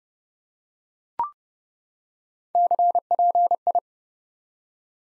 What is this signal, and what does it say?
Morse code, heard as a keyed tone, about 30 wpm, 700 Hz. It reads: CPI
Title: Top 90 Abbr 30WPM